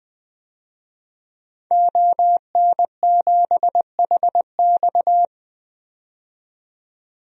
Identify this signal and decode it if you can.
Morse code, heard as a keyed tone, about 20 wpm, 700 Hz. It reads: ON7HX
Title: DXCC-100 Call Signs - 5 Characters - 20wpm